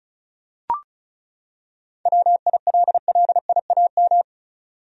Morse code, heard as a keyed tone, about 34 wpm, 700 Hz. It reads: WILLIAM
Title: USA Common Names - 35wpm